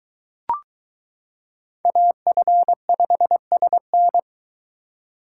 Morse code, heard as a keyed tone, about 22 wpm, 700 Hz. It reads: AF5SN